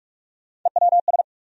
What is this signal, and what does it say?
Morse code, heard as a keyed tone, about 46 wpm, 700 Hz. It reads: EWS
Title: USA Call Sign Postfixes - 3 Letters 45wpm